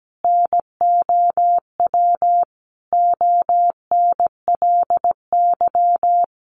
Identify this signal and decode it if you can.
Morse code, heard as a keyed tone, about 17 wpm, 700 Hz. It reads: NOW ONLY
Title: Sentences from Top 1300 Words 17wpm